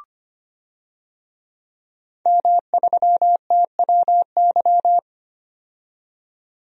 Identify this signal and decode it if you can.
Morse code, heard as a keyed tone, about 25 wpm, 700 Hz. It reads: M3TWY
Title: DXCC-100 Call Signs - 5 Characters - 25wpm